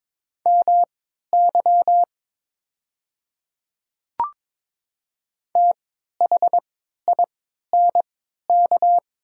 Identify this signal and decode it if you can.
Morse code, heard as a keyed tone, about 23 wpm, spaced as for 13 wpm, 700 Hz. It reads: MY THINK
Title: 3x Character Spacing - Top 300 Words - 22wpm